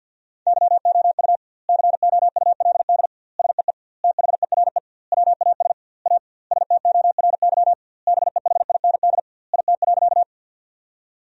Morse code, heard as a keyed tone, about 50 wpm, 700 Hz. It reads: YOU COULD SEE THERE WAS A STORY BEHIND IT.
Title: 2x Word-Spacing - Sentences from Top 500 Words - 50wpm